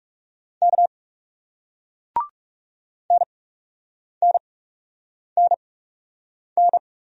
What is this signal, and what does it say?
Morse code, heard as a keyed tone, about 40 wpm, spaced as for 8 wpm, 700 Hz. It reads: K NNNN